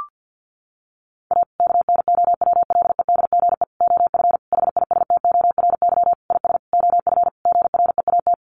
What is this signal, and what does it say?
Morse code, heard as a keyed tone, about 48 wpm, 700 Hz. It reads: A KNOWLEDGE OF HISTORY IS OF GREAT
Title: Sentences from Top 900 Words 50wpm